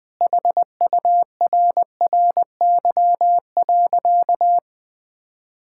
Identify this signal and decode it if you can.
Morse code, heard as a keyed tone, about 20 wpm, 700 Hz. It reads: HURRY.
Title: Sentences with 5-Word Phrases - Spoken Beginning - 20wpm